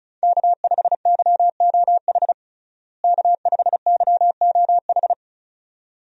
Morse code, heard as a keyed tone, about 34 wpm, 700 Hz. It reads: K5YOH K5YOH
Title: USA Call Signs - 5 Characters - Repeated 2x - 35wpm